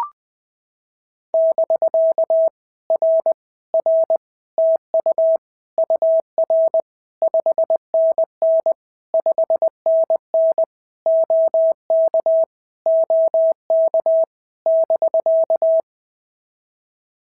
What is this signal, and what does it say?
Morse code, heard as a keyed tone, about 20 wpm, 650 Hz. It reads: <BK> R R TU UR 5NN 5NN OK OK <BK>
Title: POTA QSO Practice - 20wpm